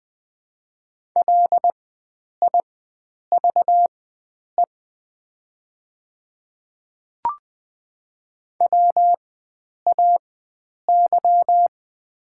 Morse code, heard as a keyed tone, about 20 wpm, spaced as for 9 wpm, 700 Hz. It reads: LIVE WAY
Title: 4x Character Spacing - Top 300 Words - 20wpm